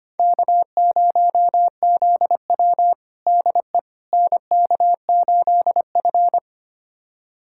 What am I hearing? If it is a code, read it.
Morse code, heard as a keyed tone, about 25 wpm, 700 Hz. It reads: K0ZW DE NK8F
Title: USA Callsign DE Callsign - 4 Characters - 25wpm